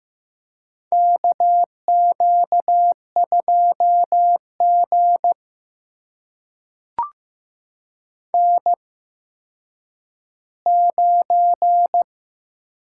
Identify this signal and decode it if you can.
Morse code, heard as a keyed tone, about 15 wpm, spaced as for 9 wpm, 700 Hz. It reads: KQ2G N9
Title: USA Call Signs - 4 Characters - 8x Character Spacing - 15wpm